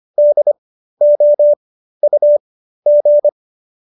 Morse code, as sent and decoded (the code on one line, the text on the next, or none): -.. --- ..- --.
DOUG